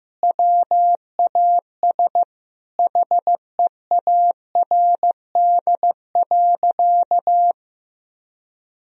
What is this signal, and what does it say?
Morse code, heard as a keyed tone, about 15 wpm, 700 Hz. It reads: WAS HEARD.
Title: Sentences from Top 400 Words and Abbreviations 15wpm